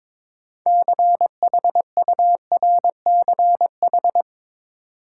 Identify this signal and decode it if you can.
Morse code, heard as a keyed tone, about 22 wpm, 700 Hz. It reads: CHURCH
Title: Sets of 5 Words - New Word Drill - 22wpm